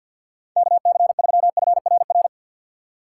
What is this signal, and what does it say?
Morse code, heard as a keyed tone, about 47 wpm, 700 Hz. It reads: KC3FRR
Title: USA Call Signs - 6 Characters - 4x Character Spacing - 50wpm